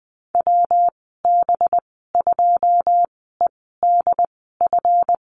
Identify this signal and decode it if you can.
Morse code, heard as a keyed tone, about 19 wpm, spaced as for 14 wpm, 700 Hz. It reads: WB2EDF
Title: 200 Lesson 32g - Letter k - Call Signs 20wpm